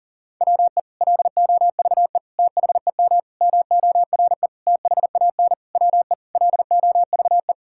Morse code, heard as a keyed tone, about 40 wpm, 700 Hz. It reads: WE LOVE THEM MORE THAN WE LOVE
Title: Sentences from Top 1600 Words 40wpm